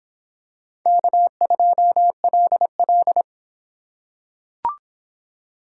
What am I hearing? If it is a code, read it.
Morse code, heard as a keyed tone, about 24 wpm, 700 Hz. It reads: K2LL